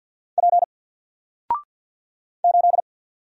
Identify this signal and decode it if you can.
Morse code, heard as a keyed tone, about 48 wpm, 700 Hz. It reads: P 8
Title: Single Letter-Number 50wpm